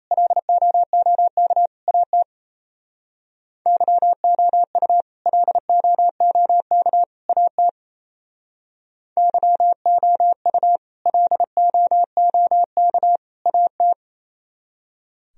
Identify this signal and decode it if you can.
Morse code, heard as a keyed tone, about 34 wpm, 700 Hz. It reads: LOOK AT YOU LOOK AT YOU LOOK AT
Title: Speed-Racing - Sets of 3 Words 28wpm